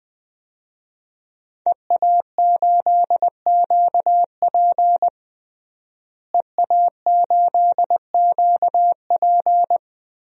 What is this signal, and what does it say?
Morse code, heard as a keyed tone, about 20 wpm, 700 Hz. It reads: EA8QP EA8QP
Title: DXCC-100 Call Signs - 5 Characters - Repeated 3x - 20wpm